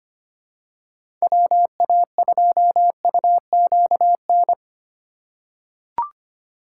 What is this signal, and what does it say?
Morse code, heard as a keyed tone, about 25 wpm, 700 Hz. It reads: WA2UQN
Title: USA Call Signs - 6 Characters - 2x Character Spacing - 25wpm